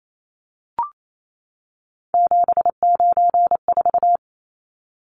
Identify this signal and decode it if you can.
Morse code, heard as a keyed tone, about 28 wpm, 700 Hz. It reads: 794
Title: Numbers - 3-Digit 28wpm